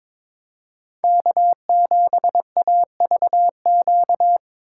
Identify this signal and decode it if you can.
Morse code, heard as a keyed tone, about 22 wpm, 700 Hz. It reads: K7AVQ
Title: USA Call Signs - 5 Characters - 6x Character Spacing - 22wpm